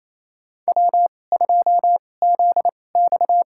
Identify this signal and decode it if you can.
Morse code, heard as a keyed tone, about 28 wpm, spaced as for 20 wpm, 690 Hz. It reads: W2ZX
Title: USA Call Signs - 2x Character Spacing - 28wpm